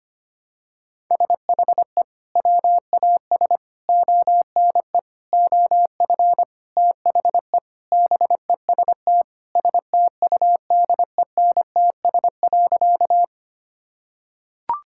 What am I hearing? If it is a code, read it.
Morse code, heard as a keyed tone, about 25 wpm, 700 Hz. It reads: SHE WAS ONE OF THE BEST STUDENTS.
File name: practice / 25wpm / Sentences with 4-Word Phrases - 25wpm